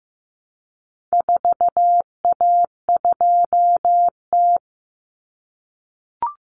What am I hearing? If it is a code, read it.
Morse code, heard as a keyed tone, about 15 wpm, 700 Hz. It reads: 4A2T
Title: Common Cut Numbers - 4-Digit 15wpm